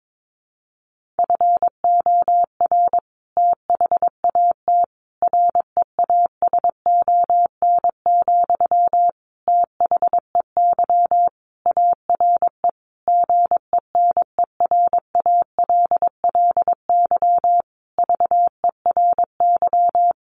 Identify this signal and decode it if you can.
Morse code, heard as a keyed tone, about 22 wpm, 700 Hz. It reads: FOR THAT REASON, THEY ARE GENERALLY VERY
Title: Sentences from Top 1500 Words 22wpm